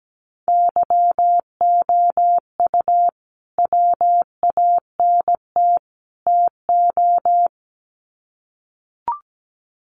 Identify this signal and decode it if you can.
Morse code, heard as a keyed tone, about 17 wpm, 700 Hz. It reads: YOU WANT TO
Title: Sets of 3 Words 17wpm